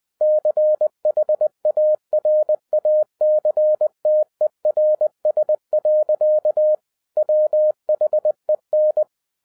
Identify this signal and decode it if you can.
Morse code, heard as a keyed tone, about 20 wpm, 610 Hz. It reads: CHARACTERS. WHEN
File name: 2011-10-mp3 / n13-TIPS FROM MY 34 YEARS OF EXPERIENCE 20wpm